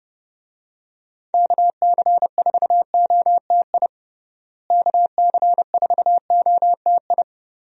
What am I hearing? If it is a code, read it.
Morse code, heard as a keyed tone, about 32 wpm, 700 Hz. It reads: KC4OTI KC4OTI
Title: USA Call Signs - 6 Characters - Repeated 2x - 30wpm